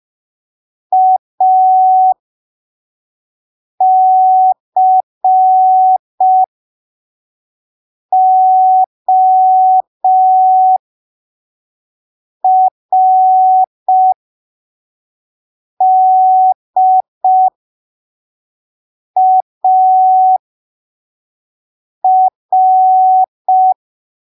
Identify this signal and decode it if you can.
Morse code, heard as a keyed tone, about 5 wpm, spaced as for 3 wpm, 750 Hz. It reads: ACORDAR
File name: arquivos / texto / texto 29 05ppm